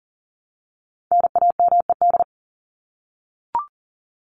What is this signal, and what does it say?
Morse code, heard as a keyed tone, about 40 wpm, 700 Hz. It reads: NAMED
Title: ED Suffix - Top 100 Words with Base Words 40wpm